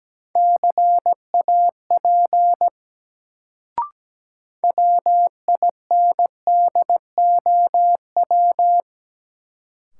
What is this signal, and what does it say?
Morse code, heard as a keyed tone, about 17 wpm, 700 Hz. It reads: CAP WINDOW
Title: Sentences from Top 1600 Words - New Word Drill - 17wpm